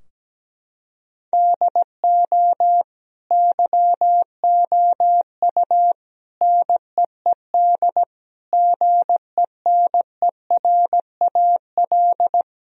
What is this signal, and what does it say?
Morse code, heard as a keyed tone, about 17 wpm, 700 Hz. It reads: DO YOU NEED GENERAL
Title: Sentences from Top 1000 Words - Review Part 3 - 17wpm